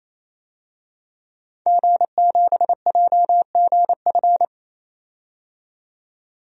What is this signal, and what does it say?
Morse code, heard as a keyed tone, about 28 wpm, 700 Hz. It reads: G7JGF
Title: DXCC-100 Call Signs - 5 Characters - 28wpm